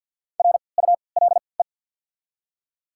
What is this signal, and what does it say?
Morse code, heard as a keyed tone, about 48 wpm, spaced as for 28 wpm, 690 Hz. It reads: RULE
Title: 3x Character Spacing - Top 500 Words - 50wpm